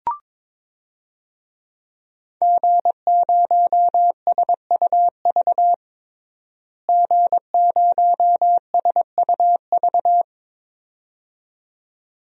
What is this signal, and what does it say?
Morse code, heard as a keyed tone, about 22 wpm, 700 Hz. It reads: G0SUV G0SUV
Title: DXCC-100 Call Signs - 5 Characters - Repeated 2x - 22wpm